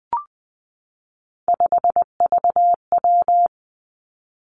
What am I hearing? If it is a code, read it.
Morse code, heard as a keyed tone, about 20 wpm, 700 Hz. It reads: 5VW
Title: Three Challenging Letter-Number Combinations 20wpm